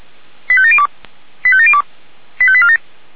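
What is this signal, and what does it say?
Five-tone selective call (ZVEI), heard as quick runs of five tones, tones of about 70 ms. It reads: 85702 85702 85847
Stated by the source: Five-tone ZVEI selective call received with an SDR.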